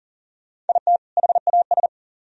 Morse code, heard as a keyed tone, about 50 wpm, 700 Hz. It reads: IT HAS